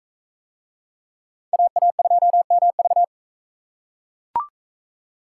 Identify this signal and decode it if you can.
Morse code, heard as a keyed tone, about 40 wpm, 700 Hz. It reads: AA2MV